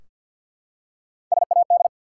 Morse code, heard as a keyed tone, about 54 wpm, 700 Hz. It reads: SAD
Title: Three Letter Words - 50wpm